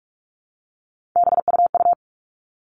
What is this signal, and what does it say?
Morse code, heard as a keyed tone, about 44 wpm, 700 Hz. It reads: BUU
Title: All Cut Numbers - 3-Digit 45wpm